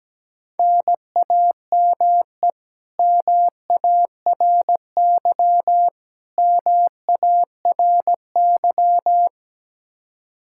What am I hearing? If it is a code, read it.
Morse code, heard as a keyed tone, about 17 wpm, 700 Hz. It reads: NAME MARY MARY
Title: K1USN SST QSO Practice 17wpm